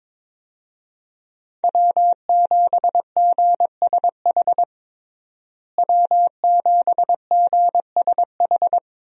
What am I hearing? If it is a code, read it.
Morse code, heard as a keyed tone, about 22 wpm, 700 Hz. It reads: W7GSH W7GSH
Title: USA Call Signs - 5 Characters - Repeated 2x - 22wpm